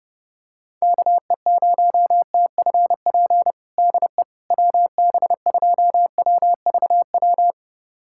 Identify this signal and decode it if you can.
Morse code, heard as a keyed tone, about 32 wpm, 690 Hz. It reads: KE0TFP DE WB2WVW
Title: USA Callsign DE Callsign - 6 Characters - 30wpm